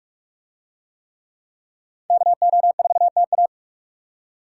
Morse code, heard as a keyed tone, about 43 wpm, 700 Hz. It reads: KO4TA